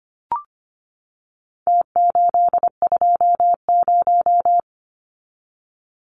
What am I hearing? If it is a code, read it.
Morse code, heard as a keyed tone, about 25 wpm, 700 Hz. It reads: T820
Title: Numbers - Contest Style - 4-Digit 25wpm